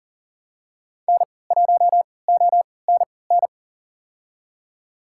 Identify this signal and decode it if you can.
Morse code, heard as a keyed tone, about 40 wpm, spaced as for 23 wpm, 700 Hz. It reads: N1ONN
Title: USA Call Signs - 5 Characters - 3x Character Spacing - 40wpm